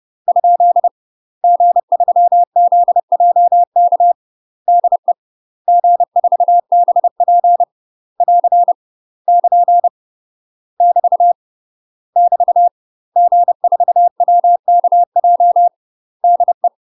Morse code, heard as a keyed tone, about 30 wpm, 700 Hz. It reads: ? G3ZJK DE G4BP <AR> <KN> <BT> <BT> G4WKJ DE
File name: UK MASTER QSOs TRAINING / 30WPM / UK MASTER QSOs 0025